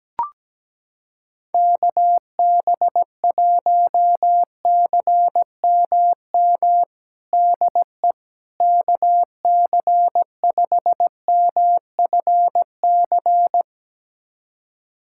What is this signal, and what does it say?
Morse code, heard as a keyed tone, about 17 wpm, 700 Hz. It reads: KB1CMM DE KC5MFC
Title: USA Callsign DE Callsign - 6 Characters - 17wpm